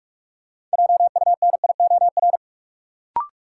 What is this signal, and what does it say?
Morse code, heard as a keyed tone, about 43 wpm, 700 Hz. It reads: JUNIOR
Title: Sentences from Top 1000 Words - Review Part 7 - New Word Drill - 45wpm